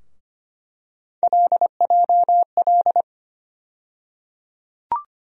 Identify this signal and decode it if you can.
Morse code, heard as a keyed tone, about 25 wpm, 700 Hz. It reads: LJL
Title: USA Call Sign Postfixes - 3 Letters 25wpm